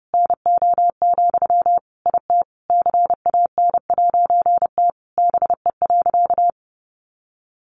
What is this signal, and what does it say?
Morse code, heard as a keyed tone, about 30 wpm, 700 Hz. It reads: NO, IT CAN'T BE.
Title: A Sound Of Thunder 30wpm